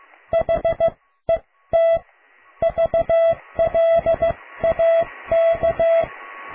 Morse code, heard as a keyed tone, about 15 wpm, spaced as for 13 wpm, 660 Hz. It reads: HET VLAK
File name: cw / tekst(a)+QRM